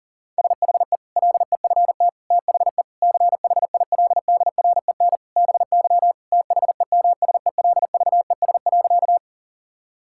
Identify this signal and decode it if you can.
Morse code, heard as a keyed tone, about 40 wpm, 700 Hz. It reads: SHE LEFT THE CHILDREN BY THEMSELVES.
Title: Sentences from Top 1000 Words - Review Part 5 - 40wpm